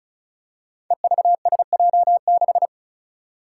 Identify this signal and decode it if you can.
Morse code, heard as a keyed tone, about 36 wpm, 700 Hz. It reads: EVSJ6
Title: Five Challenging Letter-Number Combinations 35wpm